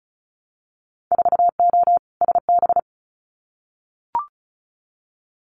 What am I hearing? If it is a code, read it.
Morse code, heard as a keyed tone, about 34 wpm, 700 Hz. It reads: 4O SB
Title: Winter Field Day Exchanges 35wpm